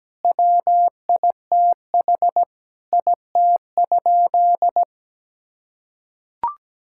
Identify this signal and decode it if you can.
Morse code, heard as a keyed tone, about 17 wpm, 700 Hz. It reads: WITH IT?
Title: Sentences from Top 1000 Words - Review Part 6 - 17wpm